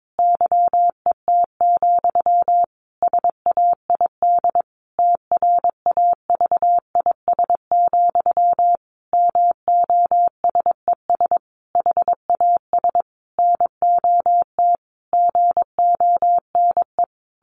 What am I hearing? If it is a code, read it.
Morse code, heard as a keyed tone, about 22 wpm, 700 Hz. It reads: YET, SAID TRAVIS, MOSES HAS NOT GONE
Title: A Sound Of Thunder 22wpm